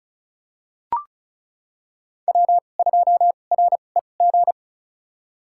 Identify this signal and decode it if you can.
Morse code, heard as a keyed tone, about 36 wpm, spaced as for 25 wpm, 700 Hz. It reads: W2REG
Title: USA Call Signs - 5 Characters - 2x Character Spacing - 35wpm